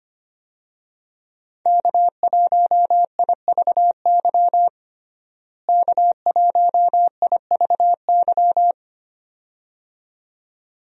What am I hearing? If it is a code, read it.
Morse code, heard as a keyed tone, about 25 wpm, 700 Hz. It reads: K1IVY K1IVY
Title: USA Call Signs - 5 Characters - Repeated 2x - 25wpm